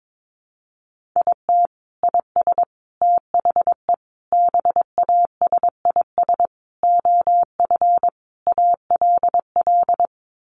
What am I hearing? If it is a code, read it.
Morse code, heard as a keyed tone, about 22 wpm, 700 Hz. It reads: IT IS THE BASIS OF ALL
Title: Sentences from Top 1200 Words 22wpm